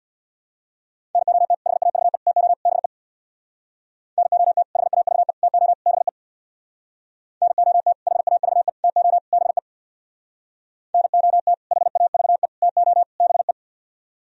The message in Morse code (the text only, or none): NOT HAVE TO BE NOT HAVE TO BE NOT HAVE TO BE NOT HAVE TO BE